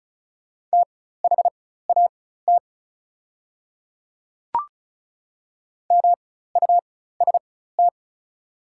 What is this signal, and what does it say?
Morse code, heard as a keyed tone, about 36 wpm, spaced as for 16 wpm, 700 Hz. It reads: THAT MUST
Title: 4x Character Spacing - Top 200 Words - 35wpm